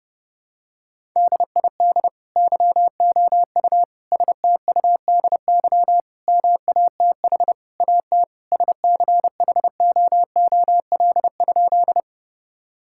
Morse code, heard as a keyed tone, about 32 wpm, 700 Hz. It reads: DID YOU STUDY MATH AT SCHOOL?
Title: Sentences from Top 1000 Words - Review Part 6 - 30wpm